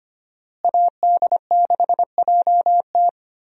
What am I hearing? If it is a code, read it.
Morse code, heard as a keyed tone, about 25 wpm, 690 Hz. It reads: AD6JT
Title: USA Call Signs - 5 Characters - Repeated 2x - 25wpm